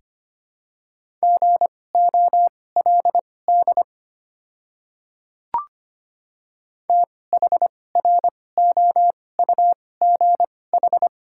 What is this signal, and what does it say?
Morse code, heard as a keyed tone, about 25 wpm, spaced as for 18 wpm, 700 Hz. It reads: GOLD THROUGH
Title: 2x Character Spacing - Top 500 Words - 25wpm